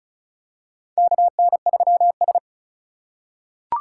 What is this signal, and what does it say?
Morse code, heard as a keyed tone, about 36 wpm, 700 Hz. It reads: KN3S
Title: USA Call Signs - 4 Characters - 3x Character Spacing - 35wpm